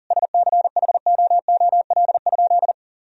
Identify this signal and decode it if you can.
Morse code, heard as a keyed tone, about 40 wpm, 700 Hz. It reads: SCHOOL?
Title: Sentences from Top 1000 Words - Review Part 1 - 40wpm